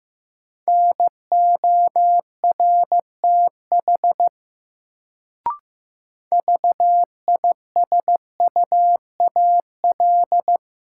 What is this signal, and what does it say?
Morse code, heard as a keyed tone, about 16 wpm, 700 Hz. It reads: NORTH VISUAL
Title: Sets of 5 Words Encore - New Word Drill - 15wpm